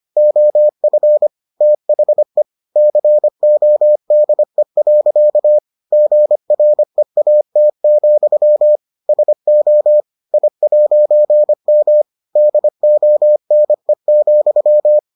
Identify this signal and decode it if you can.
Morse code, heard as a keyed tone, about 25 wpm, 600 Hz. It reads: OF THE CODE. GREAT, SO I'M DONE,